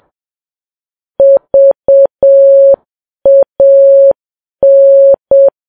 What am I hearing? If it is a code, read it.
Morse code, heard as a keyed tone, about 7 wpm, 550 Hz. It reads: VAN
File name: Files Web tls / Morse on9fv / morse7kps